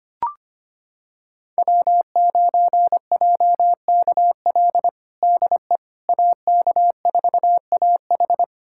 Morse code, heard as a keyed tone, about 25 wpm, 700 Hz. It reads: W9JKL DE AK4AH
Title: USA Callsign DE Callsign 25wpm